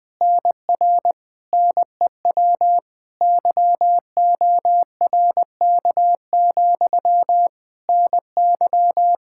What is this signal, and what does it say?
Morse code, heard as a keyed tone, about 20 wpm, 700 Hz. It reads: NR NEW YORK, NY
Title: Practice QSO - Exchange 1 - Beginning - 20wpm